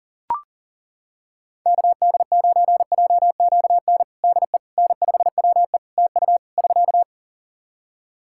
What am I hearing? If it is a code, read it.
Morse code, heard as a keyed tone, about 37 wpm, 700 Hz. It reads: KD9JQN DE N5WE TU <SK>
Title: Practice QSO - Exchange 4 - Ending - 40wpm